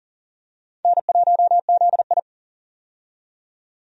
Morse code, heard as a keyed tone, about 40 wpm, 700 Hz. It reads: N1ZI
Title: USA Call Signs - 4 Characters - 40wpm